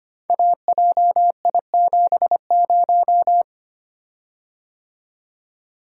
Morse code, heard as a keyed tone, about 25 wpm, 700 Hz. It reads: AJI70
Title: Code Groups - 5 Letter:Numbers - 25wpm